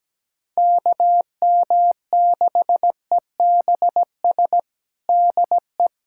Morse code, heard as a keyed tone, about 17 wpm, 700 Hz. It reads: KM6EBS DE
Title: USA Callsign DE Callsign - 6 Characters - 17wpm